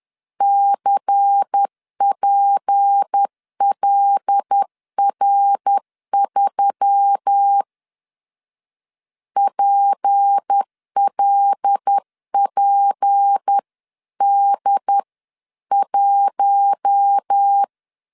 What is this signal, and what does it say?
Morse code, heard as a keyed tone, about 10 wpm, 800 Hz. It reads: CPLR3 PLPD1
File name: LEZIONI / 6 PKX0 / 6 P 10wpm 2432